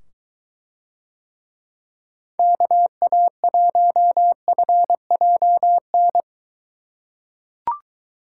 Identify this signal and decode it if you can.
Morse code, heard as a keyed tone, about 22 wpm, 700 Hz. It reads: KA1FJN